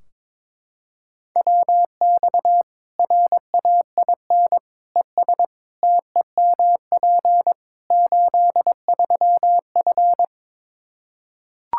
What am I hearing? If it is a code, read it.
Morse code, heard as a keyed tone, about 22 wpm, 700 Hz. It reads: WX RAIN ES TEMP 83F
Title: Practice QSO - Exchange 2 - Info - 22wpm